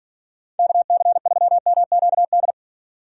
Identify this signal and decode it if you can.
Morse code, heard as a keyed tone, about 54 wpm, 700 Hz. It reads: KC3MQD